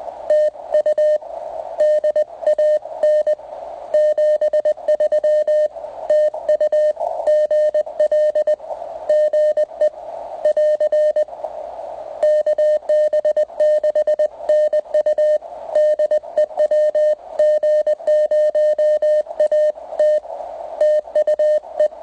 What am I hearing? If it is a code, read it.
Morse code, heard as a keyed tone, about 20 wpm, 590 Hz. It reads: TU DAN 73 TU GL GE <AR> KB6NU DE WG0AT TUE